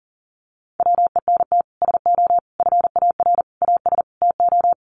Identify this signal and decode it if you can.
Morse code, heard as a keyed tone, about 40 wpm, 700 Hz. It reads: WENT SO FAR AS TO
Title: Sets of 5 Words Encore 40wpm